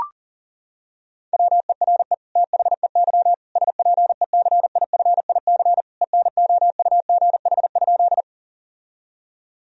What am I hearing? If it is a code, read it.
Morse code, heard as a keyed tone, about 40 wpm, 700 Hz. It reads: WERE THEY SPECIFIC ENOUGH?
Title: Sentences from Top 1000 Words - Review Part 3 - 40wpm